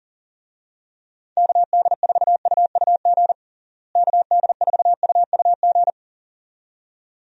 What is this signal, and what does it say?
Morse code, heard as a keyed tone, about 40 wpm, 700 Hz. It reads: KD4UUG KD4UUG
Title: USA Call Signs - 6 Characters - Repeated 2x - 40wpm